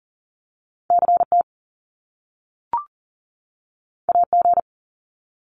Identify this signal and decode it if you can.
Morse code, heard as a keyed tone, about 39 wpm, 700 Hz. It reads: CT AG
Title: Top Two Letter Combinations in Words 40wpm